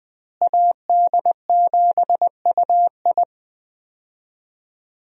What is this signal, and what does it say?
Morse code, heard as a keyed tone, about 20 wpm, 700 Hz. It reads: AD7UI
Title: USA Call Signs - 5 Characters - Repeated 3x - 20wpm